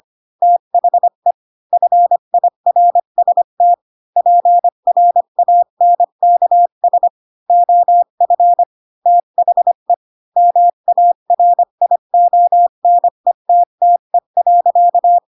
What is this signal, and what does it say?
Morse code, heard as a keyed tone, about 25 wpm, 700 Hz. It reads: THE FIRST PRANKS OF THE MARIONETTE.